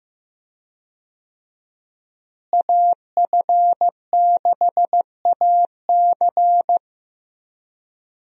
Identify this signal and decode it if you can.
Morse code, heard as a keyed tone, about 15 wpm, 700 Hz. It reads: AF6AC
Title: USA Call Signs - 5 Characters - Repeated 3x - 15wpm